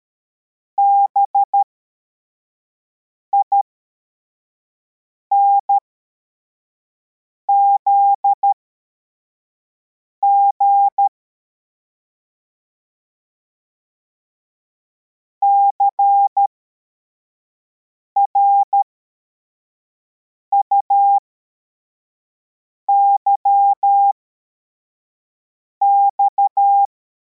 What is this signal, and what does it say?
Morse code, heard as a keyed tone, about 13 wpm, spaced as for 4 wpm, 800 Hz. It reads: BINZG CRUYX
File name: formation / cw / Text55